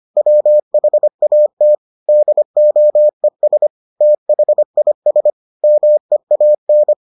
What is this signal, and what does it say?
Morse code, heard as a keyed tone, about 25 wpm, 600 Hz. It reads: WHAT DOES THIS MEAN